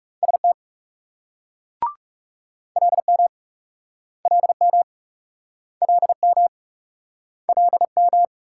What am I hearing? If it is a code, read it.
Morse code, heard as a keyed tone, about 42 wpm, 700 Hz. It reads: ST LM LM LM LM